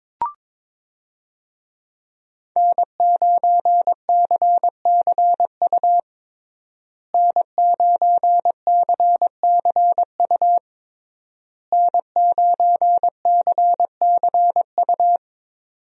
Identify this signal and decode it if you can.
Morse code, heard as a keyed tone, about 22 wpm, 700 Hz. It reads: N9CCU N9CCU N9CCU
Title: USA Call Signs - 5 Characters - Repeated 3x - 22wpm